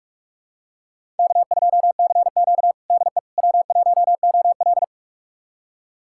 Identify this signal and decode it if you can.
Morse code, heard as a keyed tone, about 46 wpm, 700 Hz. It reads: K2CQ DE W1OL
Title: USA Callsign DE Callsign - 4 Characters - 45wpm